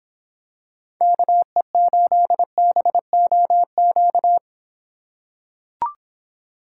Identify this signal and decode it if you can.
Morse code, heard as a keyed tone, about 26 wpm, 700 Hz. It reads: KE8BOQ